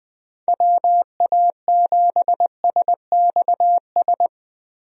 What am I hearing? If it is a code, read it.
Morse code, heard as a keyed tone, about 20 wpm, 700 Hz. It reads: WA7SXS
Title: USA Call Signs - 6 Characters - Repeated 3x - 20wpm